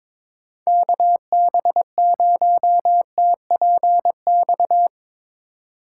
Morse code, heard as a keyed tone, about 23 wpm, 700 Hz. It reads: KB0TPX
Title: Speed Racing - USA 2x3 Call Signs 22wpm